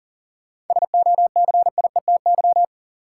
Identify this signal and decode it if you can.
Morse code, heard as a keyed tone, about 40 wpm, 700 Hz. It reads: SOCIETY
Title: Sentences from Top 700 Words - New Word Drill - 40wpm